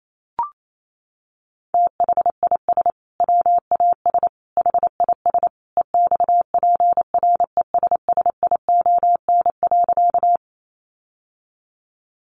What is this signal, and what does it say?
Morse code, heard as a keyed tone, about 28 wpm, 700 Hz. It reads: THIS WAS HIS EXPRESSION.
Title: Sentences from Top 1200 Words 28wpm